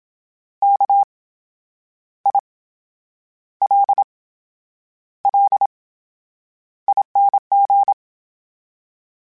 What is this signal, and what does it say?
Morse code, heard as a keyed tone, about 27 wpm, spaced as for 9 wpm, 800 Hz. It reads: KILLING
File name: Special INGs / Special INGs-mp3s / VerySlowToFaster Ing @25